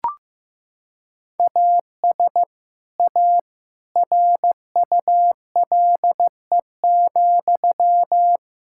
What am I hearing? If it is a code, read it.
Morse code, heard as a keyed tone, about 15 wpm, 700 Hz. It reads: AS A RULE,
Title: Sentences from Top 1000 Words - Review Part 6 - 15wpm